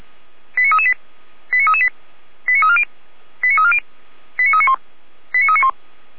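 Five-tone selective call (ZVEI), heard as quick runs of five tones, tones of about 70 ms. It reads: 89338 89338 89340 89340 89391 89391
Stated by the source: ZVEI five-tone call 89315 received by SDR, logged by Feuerwehr Orlamünde.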